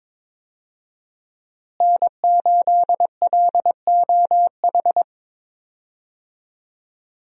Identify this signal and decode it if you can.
Morse code, heard as a keyed tone, about 22 wpm, 700 Hz. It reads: N8LOH
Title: USA Call Signs - 5 Characters - Repeated 2x - 22wpm